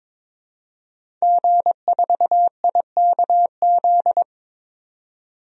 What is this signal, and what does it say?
Morse code, heard as a keyed tone, about 22 wpm, 700 Hz. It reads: G4IKZ
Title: DXCC-100 Call Signs - 5 Characters - 22wpm